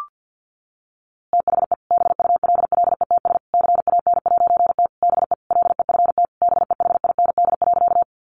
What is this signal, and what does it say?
Morse code, heard as a keyed tone, about 50 wpm, 700 Hz. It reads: THE BULLETS CAN'T BE LEFT BEHIND.
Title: A Sound Of Thunder 50wpm